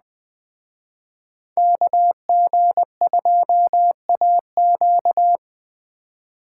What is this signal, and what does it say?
Morse code, heard as a keyed tone, about 21 wpm, 700 Hz. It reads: KG2AQ